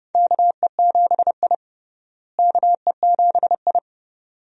Morse code, heard as a keyed tone, about 32 wpm, 700 Hz. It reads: KE7I KE7I
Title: USA Call Signs - 4 Characters - Repeated 2x - 30wpm